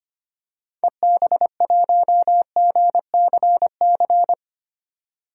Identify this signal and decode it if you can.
Morse code, heard as a keyed tone, about 25 wpm, 700 Hz. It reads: EB1GCC
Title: DXCC-100 Call Signs - 6 Characters - 25wpm